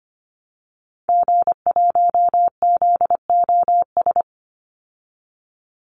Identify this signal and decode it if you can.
Morse code, heard as a keyed tone, about 25 wpm, 700 Hz. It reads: G1ZOS
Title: Call Signs Repeated 2x - 25wpm